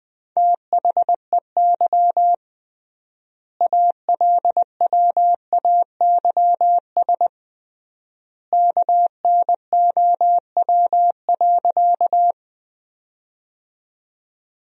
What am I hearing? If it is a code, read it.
Morse code, heard as a keyed tone, about 20 wpm, 700 Hz. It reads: THEY ALWAYS KNOW.
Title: 3x Word-Spacing - Sentences from Top 300 Words - 20wpm